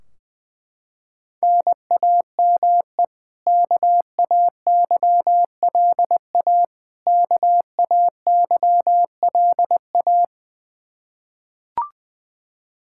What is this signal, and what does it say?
Morse code, heard as a keyed tone, about 21 wpm, 700 Hz. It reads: NAME KAYLA KAYLA
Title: Practice QSO - Exchange 1 - Beginning - 20wpm